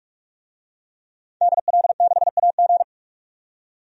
Morse code, heard as a keyed tone, about 46 wpm, 700 Hz. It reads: DL6AG
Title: DXCC-100 Call Signs - 5 Characters - 45wpm